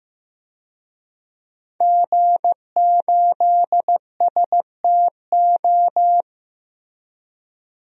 Morse code, heard as a keyed tone, about 15 wpm, 700 Hz. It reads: G8STO
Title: DXCC-100 Call Signs - 5 Characters - Repeated 3x - 15wpm